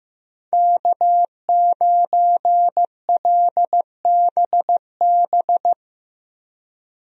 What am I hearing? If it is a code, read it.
Morse code, heard as a keyed tone, about 15 wpm, 700 Hz. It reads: K9LBB
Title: USA Call Signs - 5 Characters - Repeated 2x - 15wpm